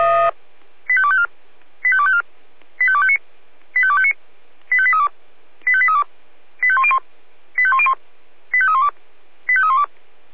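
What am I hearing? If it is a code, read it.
Five-tone selective call (ZVEI), heard as quick runs of five tones, tones of about 70 ms. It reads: 85254 85254 85259 85259 85832 85832 85101 85101 85312 85312
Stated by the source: Five-tone ZVEI selective call received on an SDR.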